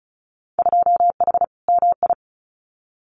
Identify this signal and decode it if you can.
Morse code, heard as a keyed tone, about 34 wpm, 700 Hz. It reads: 2H MI
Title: Winter Field Day Exchanges 35wpm